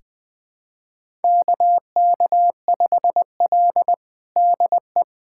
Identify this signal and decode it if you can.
Morse code, heard as a keyed tone, about 20 wpm, 700 Hz. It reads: KK5L DE
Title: USA Callsign DE Callsign - 4 Characters - 20wpm